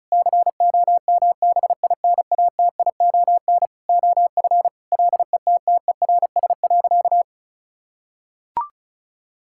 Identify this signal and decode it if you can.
Morse code, heard as a keyed tone, about 34 wpm, 700 Hz. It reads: COMBINATION OF LETTERS.
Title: Sentences from Top 1000 Words - Review Part 6 - 35wpm